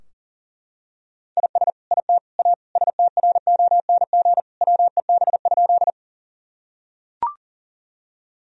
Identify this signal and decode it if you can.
Morse code, heard as a keyed tone, about 40 wpm, 700 Hz. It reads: IS IT A STRONG WEB?
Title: Sentences from Top 1000 Words - Review Part 4 - 40wpm